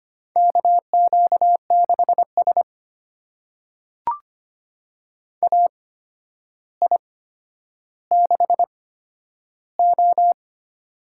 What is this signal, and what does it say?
Morse code, heard as a keyed tone, about 25 wpm, spaced as for 11 wpm, 700 Hz. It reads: KQ6S AI6O
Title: USA Call Signs - 4 Characters - 8x Character Spacing - 25wpm